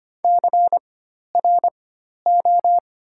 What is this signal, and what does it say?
Morse code, heard as a keyed tone, about 25 wpm, spaced as for 12 wpm, 700 Hz. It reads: CRO
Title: 4x Character Spacing - Top 300 Words - 25wpm